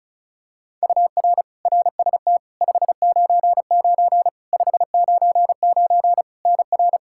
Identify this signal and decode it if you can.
Morse code, heard as a keyed tone, about 34 wpm, 700 Hz. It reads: UR RST 599 599 NR
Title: Practice QSO - Exchange 1 - Beginning - 35wpm